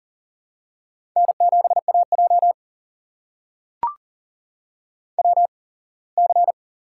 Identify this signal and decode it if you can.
Morse code, heard as a keyed tone, about 43 wpm, spaced as for 24 wpm, 700 Hz. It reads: N7AJ WC
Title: USA Call Signs - 4 Characters - 8x Character Spacing - 40wpm